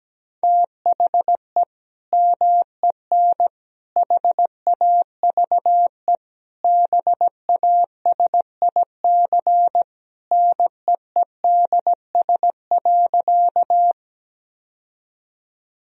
Morse code, heard as a keyed tone, about 17 wpm, 700 Hz. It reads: THE MEN HAVE BASIC NEEDS.
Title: Sentences from Top 1000 Words - Review Part 3 - 17wpm